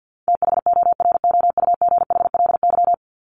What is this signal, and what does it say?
Morse code, heard as a keyed tone, about 50 wpm, 700 Hz. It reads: THOROUGHLY
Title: LY Suffix - Top 100 Words with Base Words 50wpm